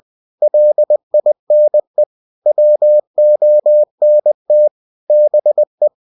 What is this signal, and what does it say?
Morse code, heard as a keyed tone, about 20 wpm, 600 Hz. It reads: LINE WONT BE
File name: cw-books / winnie-the-pooh / winnie-20wpm-page-0076